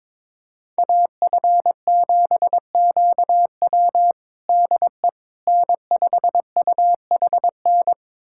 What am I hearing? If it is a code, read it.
Morse code, heard as a keyed tone, about 22 wpm, 700 Hz. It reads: AF7QW DE N5UHN
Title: USA Callsign DE Callsign - 5 Characters - 22wpm